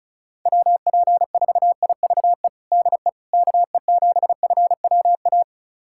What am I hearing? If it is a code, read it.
Morse code, heard as a keyed tone, about 34 wpm, 700 Hz. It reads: WP4IVE DE KE7FWA
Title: USA Callsign DE Callsign 35wpm